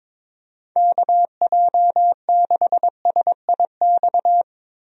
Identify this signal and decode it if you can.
Morse code, heard as a keyed tone, about 22 wpm, 700 Hz. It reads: KJ6SIX
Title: USA Call Signs - 6 Characters - Repeated 2x - 22wpm